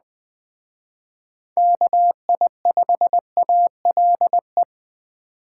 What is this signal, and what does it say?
Morse code, heard as a keyed tone, about 21 wpm, 700 Hz. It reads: KI5ALE
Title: Speed Racing - USA 2x3 Call Signs 17wpm